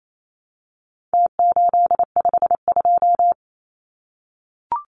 Morse code, heard as a keyed tone, about 28 wpm, 700 Hz. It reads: T852
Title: Numbers - Contest Style - 4-Digit 28wpm